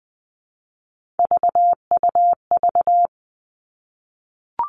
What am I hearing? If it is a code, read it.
Morse code, heard as a keyed tone, about 20 wpm, 700 Hz. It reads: VUV
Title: Three Challenging Letter-Number Combinations 20wpm